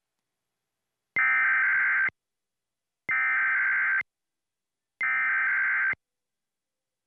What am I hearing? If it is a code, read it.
Emergency Alert System (SAME) tones (header bursts), heard as harsh buzzing bursts. It is heard from 1.2 to 5.9 s.